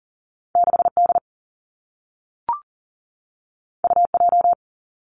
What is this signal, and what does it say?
Morse code, heard as a keyed tone, about 42 wpm, 700 Hz. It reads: 6 D U J
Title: Two Challenging Letter-Number Combinations 40wpm